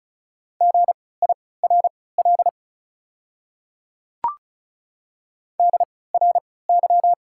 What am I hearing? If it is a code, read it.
Morse code, heard as a keyed tone, about 36 wpm, spaced as for 20 wpm, 690 Hz. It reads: GIRL DRY
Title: 3x Character Spacing - Top 500 Words - 35wpm